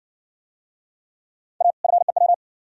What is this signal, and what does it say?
Morse code, heard as a keyed tone, about 60 wpm, 700 Hz. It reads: A FEW